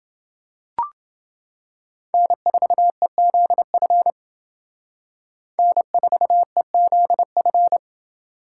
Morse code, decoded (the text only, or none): N4EZF N4EZF